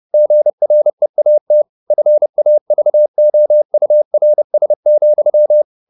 Morse code, heard as a keyed tone, about 30 wpm, 600 Hz. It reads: GREAT FAVOURS,
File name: the old man and the sea / the old man and the sea 1 30wpm